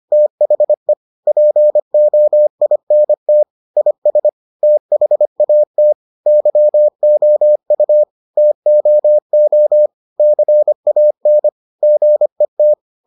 Morse code, heard as a keyed tone, about 25 wpm, 600 Hz. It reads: THE POINT IS THAT YOU TOO CAN GET